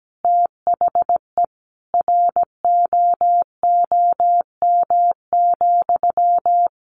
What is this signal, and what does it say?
Morse code, heard as a keyed tone, about 17 wpm, 700 Hz. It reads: THE ROOM,
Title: Sentences from Top 1400 Words 17wpm